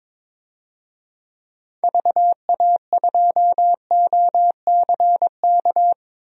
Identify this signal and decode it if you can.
Morse code, heard as a keyed tone, about 22 wpm, 700 Hz. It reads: VA2OCK
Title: DXCC-100 Call Signs - 6 Characters - 22wpm